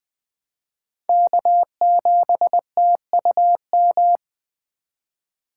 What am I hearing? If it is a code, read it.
Morse code, heard as a keyed tone, about 20 wpm, 700 Hz. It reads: K7TUM
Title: USA Call Signs - 5 Characters - 3x Character Spacing - 20wpm